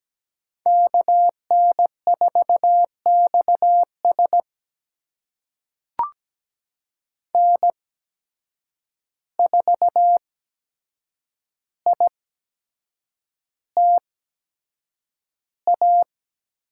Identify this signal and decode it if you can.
Morse code, heard as a keyed tone, about 17 wpm, spaced as for 7 wpm, 700 Hz. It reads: KN4XS N4ITA
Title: USA Call Signs - 5 Characters - 8x Character Spacing - 17wpm